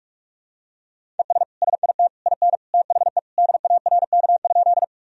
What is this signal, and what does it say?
Morse code, heard as a keyed tone, about 44 wpm, 700 Hz. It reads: ES SIT IN THE DARK?
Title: Sentences from Top 500 Words and Abbreviations 45wpm